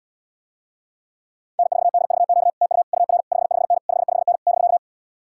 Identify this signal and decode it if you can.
Morse code, heard as a keyed tone, about 78 wpm, 700 Hz. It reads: N4DVQ TU UR 55N 55N <BK>